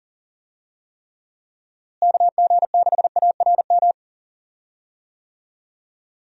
Morse code, heard as a keyed tone, about 40 wpm, 700 Hz. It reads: KG6ARM
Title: USA Call Signs - 6 Characters - Repeated 2x - 40wpm